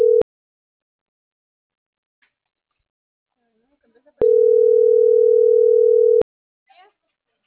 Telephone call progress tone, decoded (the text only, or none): ringback tone